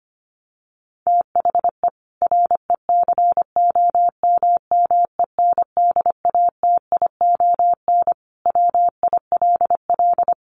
Morse code, heard as a keyed tone, about 25 wpm, 700 Hz. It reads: THE RECOMMENDATION WILL
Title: Sentences from Top 1400 Words 25wpm